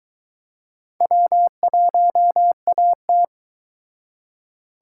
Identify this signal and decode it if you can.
Morse code, heard as a keyed tone, about 22 wpm, 700 Hz. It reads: W1AT